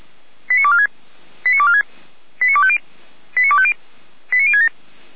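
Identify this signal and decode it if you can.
Five-tone selective call (ZVEI), heard as quick runs of five tones, tones of about 70 ms. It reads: 89246 89246 89250 89250 89067
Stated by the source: ZVEI five-tone selective call 89108, received on an SDR, from the feuerwehr-orlamuende fire brigade archive.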